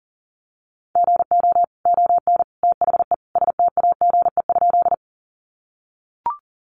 Morse code, heard as a keyed tone, about 40 wpm, 700 Hz. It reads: GO ON THE STAGE?
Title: Sentences from Top 1200 Words 40wpm